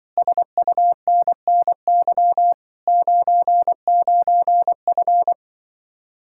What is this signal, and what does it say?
Morse code, heard as a keyed tone, about 24 wpm, 700 Hz. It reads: SUNNY 99F
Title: Speed-Racing Common QSO Phrases - 20wpm